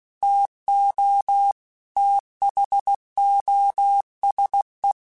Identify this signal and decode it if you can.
Morse code, heard as a keyed tone, about 16 wpm, 800 Hz. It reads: TO THOSE
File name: Technical Subjects / Technical Subjects-mp3s / XL @15